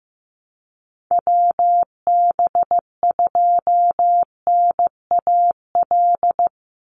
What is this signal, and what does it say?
Morse code, heard as a keyed tone, about 15 wpm, 700 Hz. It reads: WB2NAL
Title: USA Call Signs 15wpm